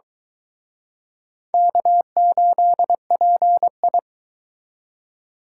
Morse code, heard as a keyed tone, about 23 wpm, 700 Hz. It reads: K8PI